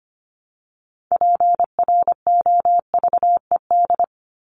Morse code, heard as a keyed tone, about 25 wpm, 700 Hz. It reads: PROVED
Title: ED Suffix - Top 100 Words with Base Words 25wpm